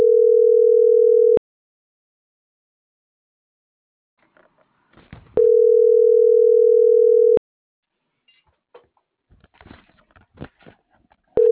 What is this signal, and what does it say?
Ringback tone (North America), 440 + 480 Hz.